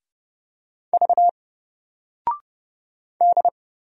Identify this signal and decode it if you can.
Morse code, heard as a keyed tone, about 31 wpm, 700 Hz. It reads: V D
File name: Single-Letter-Number-30wpm